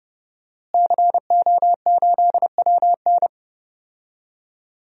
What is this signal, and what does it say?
Morse code, heard as a keyed tone, about 30 wpm, 700 Hz. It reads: CO8WN
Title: DXCC-100 Call Signs - 5 Characters - Repeated 2x - 30wpm